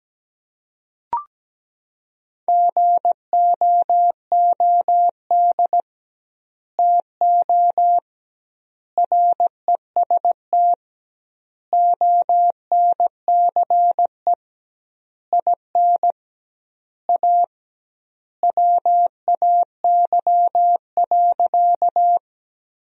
Morse code, heard as a keyed tone, about 17 wpm, 700 Hz. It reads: GOOD TO REST ONCE IN A WAY.
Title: 2x Word-Spacing - Sentences from Top 500 Words - 17wpm